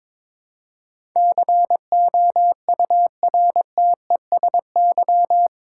Morse code, heard as a keyed tone, about 22 wpm, 700 Hz. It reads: COURTESY
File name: practice / 22wpm / Eight Letter Words - 22wpm